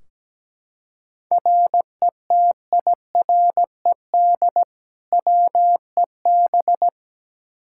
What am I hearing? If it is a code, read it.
Morse code, heard as a keyed tone, about 17 wpm, 700 Hz. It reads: RETIRED WEB
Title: K1USN SST QSO Practice 17wpm